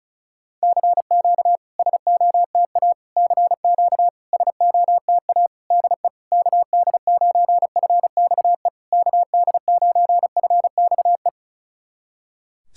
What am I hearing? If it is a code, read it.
Morse code, heard as a keyed tone, about 36 wpm, 700 Hz. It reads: CQ SOTA CQ SOTA DE KD9FXE KD9FXE
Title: SOTA QSO Practice 35wpm